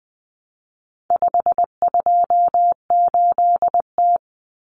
Morse code, heard as a keyed tone, about 20 wpm, 700 Hz. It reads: 528T
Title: Common Cut Numbers - 4-Digit 20wpm